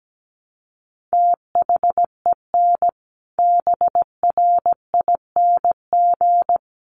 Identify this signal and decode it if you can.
Morse code, heard as a keyed tone, about 17 wpm, 700 Hz. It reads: THEN BRING
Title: Sentences from Top 500 Words 17wpm